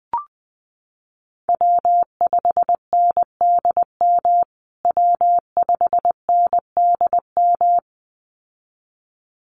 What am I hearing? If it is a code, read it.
Morse code, heard as a keyed tone, about 20 wpm, 700 Hz. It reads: W5NDM W5NDM
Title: USA Call Signs Repeated 2x 20wpm